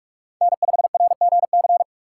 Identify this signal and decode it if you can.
Morse code, heard as a keyed tone, about 42 wpm, 690 Hz. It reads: N5RGC
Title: USA Call Signs - 5 Characters - 6x Character Spacing - 45wpm